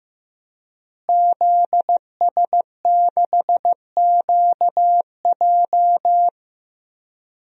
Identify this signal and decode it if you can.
Morse code, heard as a keyed tone, about 15 wpm, 700 Hz. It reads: ZS6QJ
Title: DXCC-100 Call Signs - 5 Characters - Repeated 2x - 15wpm